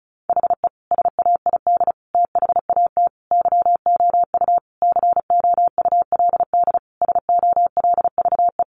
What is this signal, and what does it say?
Morse code, heard as a keyed tone, about 34 wpm, 700 Hz. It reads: HE SAID THAT YOU COULD SOLVE
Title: Sentences from Top 1000 Words 35wpm